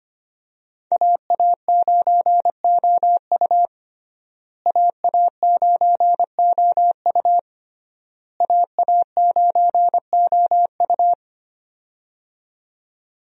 Morse code, heard as a keyed tone, about 25 wpm, 700 Hz. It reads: AA9OU AA9OU AA9OU
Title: USA Call Signs - 5 Characters - Repeated 3x - 25wpm